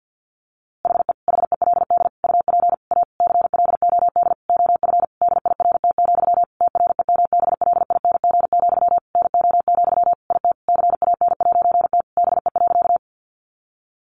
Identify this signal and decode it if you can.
Morse code, heard as a keyed tone, about 50 wpm, 700 Hz. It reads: HE HELD UP A CLOD OF DIRT, TREMBLING, NO, IT CAN'T BE.
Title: A Sound Of Thunder 50wpm